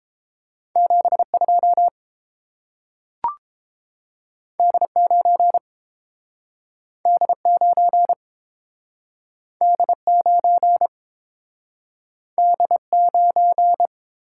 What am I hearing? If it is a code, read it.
Morse code, heard as a keyed tone, about 30 wpm, 700 Hz. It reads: 72 D9 D9 D9 D9